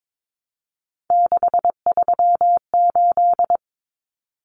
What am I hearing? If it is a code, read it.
Morse code, heard as a keyed tone, about 22 wpm, 700 Hz. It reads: 638
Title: Numbers - 3-Digit 22wpm